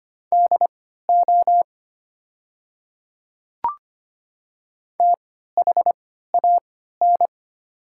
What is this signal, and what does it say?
Morse code, heard as a keyed tone, about 25 wpm, spaced as for 14 wpm, 700 Hz. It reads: DO THAN
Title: 3x Character Spacing - Top 100 Words - 25wpm